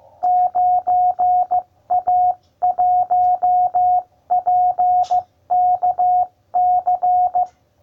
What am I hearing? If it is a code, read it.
Morse code, heard as a keyed tone, about 15 wpm, 720 Hz. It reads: 9A1PKC